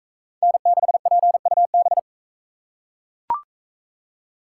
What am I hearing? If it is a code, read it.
Morse code, heard as a keyed tone, about 43 wpm, 700 Hz. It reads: N6PUB